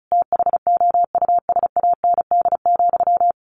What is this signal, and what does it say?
Morse code, heard as a keyed tone, about 34 wpm, 700 Hz. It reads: THOUSAND,
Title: A Sound Of Thunder 35wpm